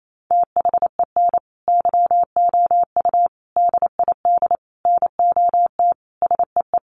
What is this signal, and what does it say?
Morse code, heard as a keyed tone, about 28 wpm, 700 Hz. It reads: THEN YOU DID NOT SEE
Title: Sentences from Top 600 Words 28wpm